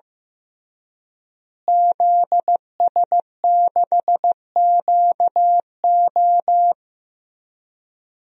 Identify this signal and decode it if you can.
Morse code, heard as a keyed tone, about 15 wpm, 700 Hz. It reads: ZS6QO
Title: DXCC-100 Call Signs - 5 Characters - Repeated 3x - 15wpm